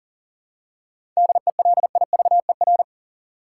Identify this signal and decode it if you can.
Morse code, heard as a keyed tone, about 40 wpm, 700 Hz. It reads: DELIVER
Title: Seven Letter Words - 40wpm